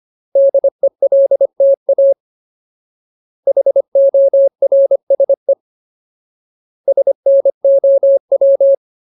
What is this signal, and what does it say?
Morse code, heard as a keyed tone, about 25 wpm, 550 Hz. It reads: DELTA HORSE SNOW
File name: Ultimate Word List / 25WPM Ultimate